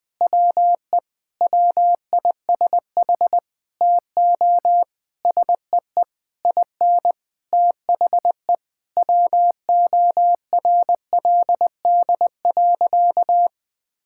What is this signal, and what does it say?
Morse code, heard as a keyed tone, about 21 wpm, 700 Hz. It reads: WE WISH TO SEE IN THE WORLD.
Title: Sentences with 9-Word Phrases - Spoken Beginning - 20wpm